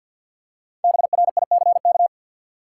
Morse code, heard as a keyed tone, about 48 wpm, 700 Hz. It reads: BRICK
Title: Five Letter Words - 50wpm